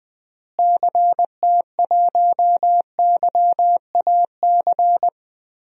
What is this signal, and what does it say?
Morse code, heard as a keyed tone, about 20 wpm, 700 Hz. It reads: CT1YAC
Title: DXCC-100 Call Signs - 6 Characters - Repeated 2x - 20wpm